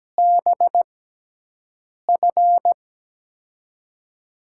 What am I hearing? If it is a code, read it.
Morse code, heard as a keyed tone, about 17 wpm, 700 Hz. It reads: B F
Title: USA Call Signs - 6x Character Spacing - 17wpm